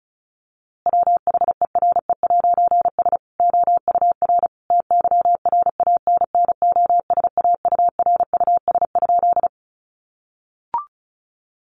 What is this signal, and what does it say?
Morse code, heard as a keyed tone, about 34 wpm, 700 Hz. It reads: WHERE'S OUR TYRANNOSAURUS?
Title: A Sound Of Thunder 35wpm